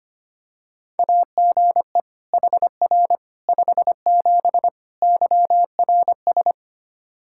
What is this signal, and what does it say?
Morse code, heard as a keyed tone, about 25 wpm, 700 Hz. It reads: AGE HR 57 YRS
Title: Practice QSO - Exchange 3 - More Info - 25wpm